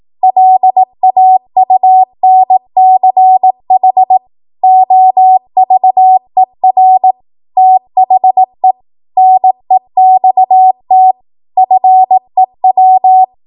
Morse code, heard as a keyed tone, about 18 wpm, 750 Hz. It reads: LAUNCH OVER THE NEXT FEW
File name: Archive / 18 WPM / 18WPM